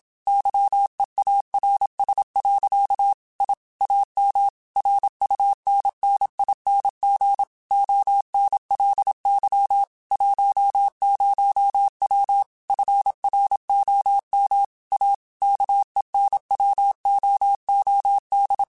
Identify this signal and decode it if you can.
Morse code, heard as a keyed tone, about 27 wpm, 800 Hz. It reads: YEARS. I AM RUNNING ONLY 10W FROM A KENWOOD